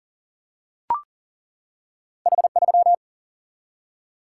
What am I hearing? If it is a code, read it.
Morse code, heard as a keyed tone, about 38 wpm, 700 Hz. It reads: H3
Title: Single-Letter Single-Number 40wpm